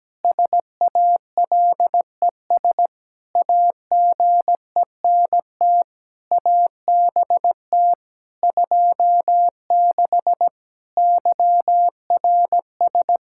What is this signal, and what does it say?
Morse code, heard as a keyed tone, about 17 wpm, 700 Hz. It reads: SALES AGENT ABT 26 YRS
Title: Practice QSO - Exchange 3 - More Info - 17wpm